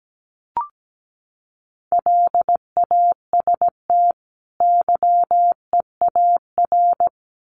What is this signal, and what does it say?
Morse code, heard as a keyed tone, about 17 wpm, 700 Hz. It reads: LAST YEAR
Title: Sets of 2 Words 17wpm